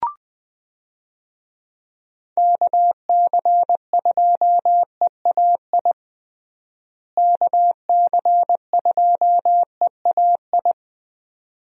Morse code, heard as a keyed tone, about 20 wpm, 700 Hz. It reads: KC2EAI KC2EAI
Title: USA Call Signs - 6 Characters - Repeated 2x - 20wpm